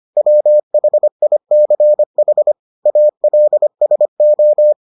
Morse code, heard as a keyed tone, about 25 wpm, 600 Hz. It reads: WHICH ALSO